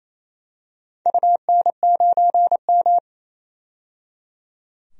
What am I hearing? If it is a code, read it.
Morse code, heard as a keyed tone, about 28 wpm, 700 Hz. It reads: UN9M
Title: DXCC-100 Call Signs - 4 Characters - 28wpm